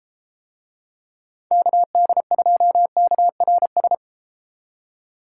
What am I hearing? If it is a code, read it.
Morse code, heard as a keyed tone, about 33 wpm, 700 Hz. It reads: KD2KRS